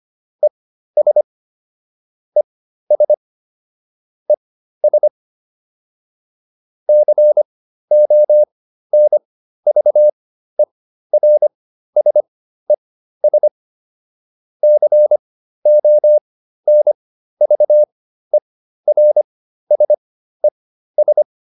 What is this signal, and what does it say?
Morse code, heard as a keyed tone, about 25 wpm, spaced as for 13 wpm, 600 Hz. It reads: ES ES ES CONVERSES CONVERSES